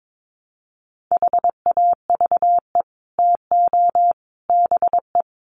Morse code, heard as a keyed tone, about 22 wpm, 700 Hz. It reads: HAVE TO BE
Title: Sets of 3 Words 22wpm